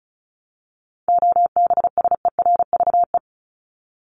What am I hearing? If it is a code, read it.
Morse code, heard as a keyed tone, about 34 wpm, 700 Hz. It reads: OBSERVE
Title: ED Suffix - Top 100 Words with Base Words 35wpm